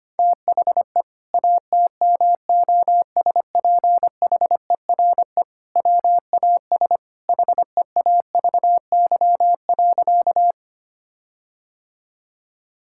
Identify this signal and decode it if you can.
Morse code, heard as a keyed tone, about 25 wpm, 700 Hz. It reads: THE ATMOSPHERE WAS HEAVY.
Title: Sentences from Top 1600 Words 25wpm